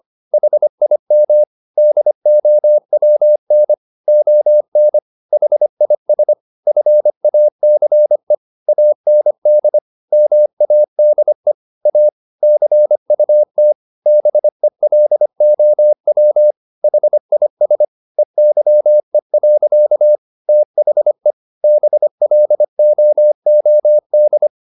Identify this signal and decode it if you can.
Morse code, heard as a keyed tone, about 25 wpm, 600 Hz. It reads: HIM DOWN ON HIS FACE AND MADE A CUT BELOW HIS EYE. THE BLOOD
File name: the old man and the sea part 1 25wpm